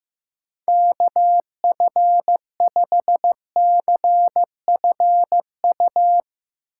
Morse code, heard as a keyed tone, about 15 wpm, 700 Hz. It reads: KF5CFU
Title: USA Call Signs - 6 Characters - 3x Character Spacing - 15wpm